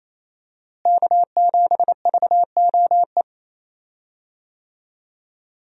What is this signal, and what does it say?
Morse code, heard as a keyed tone, about 28 wpm, 700 Hz. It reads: K7VOE
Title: USA Call Signs - 5 Characters - Repeated 2x - 28wpm